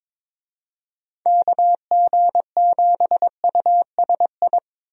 Morse code, heard as a keyed tone, about 22 wpm, 700 Hz. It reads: KG7USI
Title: USA Call Signs - 6 Characters - 22wpm